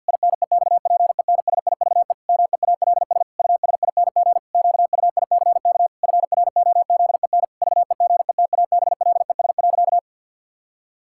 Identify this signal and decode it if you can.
Morse code, heard as a keyed tone, about 51 wpm, 700 Hz. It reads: INEXPENSIVE MEALS USING QUICK FROZEN VEGETABLES.
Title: Pangram Challenge v2 - 50wpm